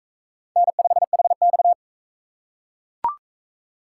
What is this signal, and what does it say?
Morse code, heard as a keyed tone, about 38 wpm, 700 Hz. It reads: N5HX